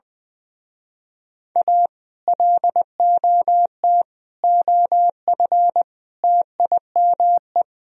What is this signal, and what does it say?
Morse code, heard as a keyed tone, about 21 wpm, 700 Hz. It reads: A LOT OF TIME
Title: Speed-Racing - Sets of 4 Words 20wpm